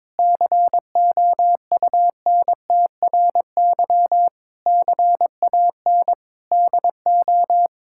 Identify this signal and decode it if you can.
Morse code, heard as a keyed tone, about 22 wpm, 700 Hz. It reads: COUNTRY CAN DO
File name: Sets of 9 Words - 22wpm